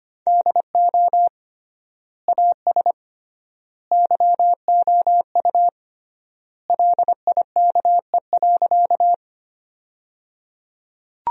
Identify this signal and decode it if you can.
Morse code, heard as a keyed tone, about 25 wpm, 700 Hz. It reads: DO AS YOU LIKE.
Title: 3x Word-Spacing - Sentences from Top 100 Words - 25wpm